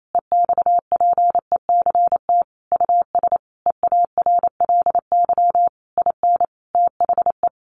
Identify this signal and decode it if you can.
Morse code, heard as a keyed tone, about 28 wpm, 700 Hz. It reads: EXPECT US EARLY IN THE
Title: Sentences from Top 1500 Words 28wpm